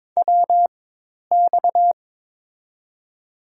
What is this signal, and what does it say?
Morse code, heard as a keyed tone, about 22 wpm, spaced as for 10 wpm, 700 Hz. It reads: WX
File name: Top 90 QSO Elements - 4x Character Spacing - 22wpm